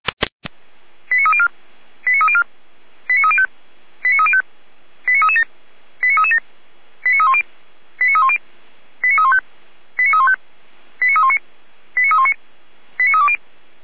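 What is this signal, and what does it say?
Five-tone selective call (ZVEI), heard as quick runs of five tones, tones of about 70 ms. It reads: 89304 89304 89305 89305 89337 89337 89310 89310 89315 89315 89319 89319 89320
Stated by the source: ZVEI five-tone call received on an SDR, from the feuerwehr-orlamuende.org archive.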